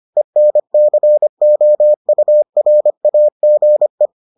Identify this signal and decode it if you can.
Morse code, heard as a keyed tone, about 25 wpm, 600 Hz. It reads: ENCOURAGE